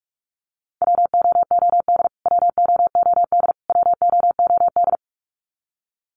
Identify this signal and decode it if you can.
Morse code, heard as a keyed tone, about 44 wpm, 700 Hz. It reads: WOOD WOOD WOOD
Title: Top 500 Words Repeated 45wpm